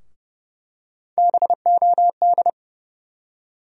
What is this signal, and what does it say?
Morse code, heard as a keyed tone, about 30 wpm, 700 Hz. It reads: BOD
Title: USA Call Sign Postfixes - 3 Letters 30wpm